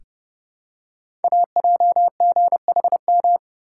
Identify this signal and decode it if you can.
Morse code, heard as a keyed tone, about 29 wpm, 700 Hz. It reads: AJGHM
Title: Code Groups - 5 Letters - 30wpm